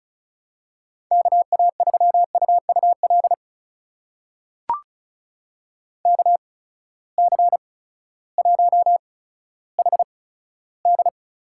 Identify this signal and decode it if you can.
Morse code, heard as a keyed tone, about 36 wpm, spaced as for 16 wpm, 700 Hz. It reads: KA3UUL KC1HD
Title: USA Call Signs - 8x Character Spacing - 35wpm